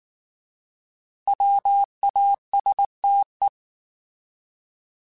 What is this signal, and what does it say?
Morse code, heard as a keyed tone, about 19 wpm, 800 Hz. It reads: WASTE
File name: Alphabetical / Alphabetical mp3s / Ws @18